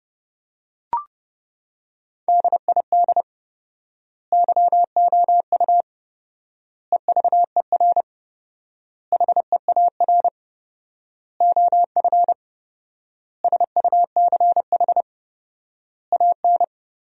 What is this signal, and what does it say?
Morse code, heard as a keyed tone, about 30 wpm, 700 Hz. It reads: DID YOU EVER HEAR OF SUCH AN
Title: 4x Word-Spacing - Sentences from Top 400 Words - 30wpm